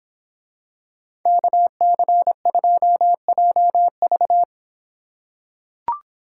Morse code, heard as a keyed tone, about 24 wpm, 700 Hz. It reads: KC2JV